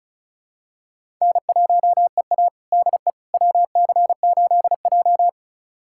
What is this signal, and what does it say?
Morse code, heard as a keyed tone, about 34 wpm, 700 Hz. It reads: N1EA DE WC8J
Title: USA Callsign DE Callsign - 4 Characters - 35wpm